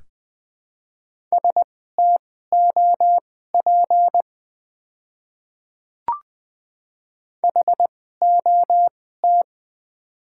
Morse code, heard as a keyed tone, about 20 wpm, spaced as for 15 wpm, 700 Hz. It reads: STOP HOT
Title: 2x Character Spacing - Top 500 Words - 20wpm